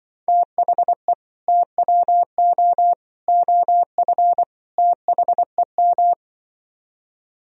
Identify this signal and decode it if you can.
Morse code, heard as a keyed tone, about 25 wpm, 700 Hz. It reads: THE TWO OF THEM